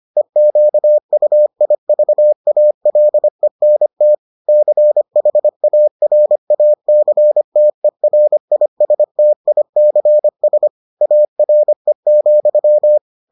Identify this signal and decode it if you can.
Morse code, heard as a keyed tone, about 25 wpm, 600 Hz. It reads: EQUIVALENT CHARACTERISTICS ARE,